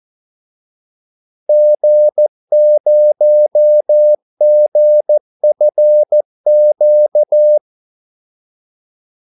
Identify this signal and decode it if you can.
Morse code, heard as a keyed tone, about 14 wpm, 600 Hz. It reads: G0GFQ
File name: Callsigns-14wpm